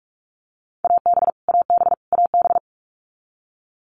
Morse code, heard as a keyed tone, about 44 wpm, 700 Hz. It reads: AB AB AB
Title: Canadian Provinces and Territories - Repeated 3x - 45wpm